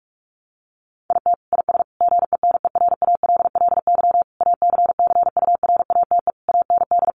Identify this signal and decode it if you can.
Morse code, heard as a keyed tone, about 44 wpm, 700 Hz. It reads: IT IS GENERALLY ACCURATE AND
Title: Sentences from Top 1200 Words 45wpm